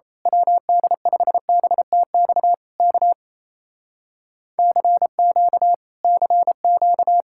Morse code, heard as a keyed tone, about 27 wpm, 700 Hz. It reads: WD5BTX K CQ CQ
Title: Speed-Racing Common QSO Phrases - 28wpm